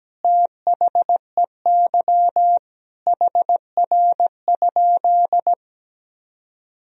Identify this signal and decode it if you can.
Morse code, heard as a keyed tone, about 17 wpm, 700 Hz. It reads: THEY HR?
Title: Sentences from Top 200 Words and Abbreviations 17wpm